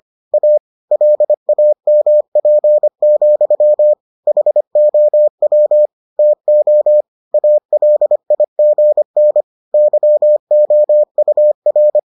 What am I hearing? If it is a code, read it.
Morse code, heard as a keyed tone, about 25 wpm, 600 Hz. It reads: A LAMP, HOW TO ALIGN YOUR